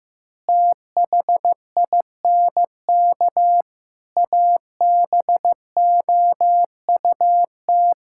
Morse code, heard as a keyed tone, about 15 wpm, 700 Hz. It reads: THINK ABOUT
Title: Sentences from Top 1000 Words - Review Part 5 - 15wpm